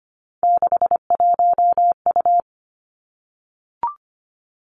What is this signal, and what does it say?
Morse code, heard as a keyed tone, about 25 wpm, 700 Hz. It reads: 61U
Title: Three Challenging Letter-Number Combinations 25wpm